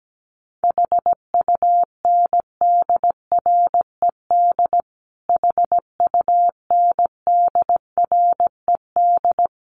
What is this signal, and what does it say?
Morse code, heard as a keyed tone, about 17 wpm, 700 Hz. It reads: HUNDRED HUNDRED
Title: Top 500 Words Repeated 17wpm